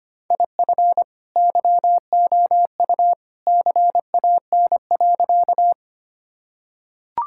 Morse code, heard as a keyed tone, about 25 wpm, 700 Hz. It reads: IF YOU CAN.
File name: practice / 25wpm / Sentences with 2-Word Phrases - 25wpm